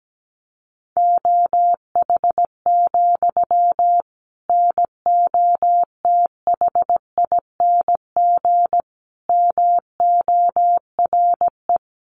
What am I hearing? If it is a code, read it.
Morse code, heard as a keyed tone, about 17 wpm, 700 Hz. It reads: OH, NOTHING MORE
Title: Sentences from Top 700 Words 17wpm